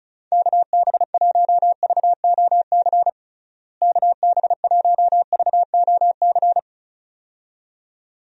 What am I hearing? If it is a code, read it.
Morse code, heard as a keyed tone, about 36 wpm, 700 Hz. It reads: KB1VOC KB1VOC
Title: USA Call Signs - 6 Characters - Repeated 2x - 35wpm